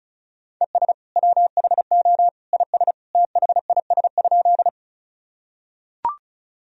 Morse code, heard as a keyed tone, about 36 wpm, 700 Hz. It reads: ES WHO IS THIS?
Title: Sentences from Top 100 Words and Abbreviations 35wpm